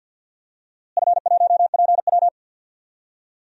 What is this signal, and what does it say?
Morse code, heard as a keyed tone, about 51 wpm, 700 Hz. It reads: F1PW
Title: DXCC-100 Call Signs - 4 Characters - Repeated 2x - 50wpm